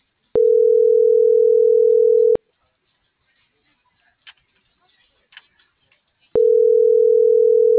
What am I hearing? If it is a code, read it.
Ringback tone (North America), 440 + 480 Hz.